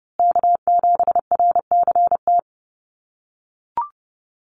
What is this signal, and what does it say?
Morse code, heard as a keyed tone, about 30 wpm, 700 Hz. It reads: K7RCT
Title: USA Call Signs Repeated 3x 30wpm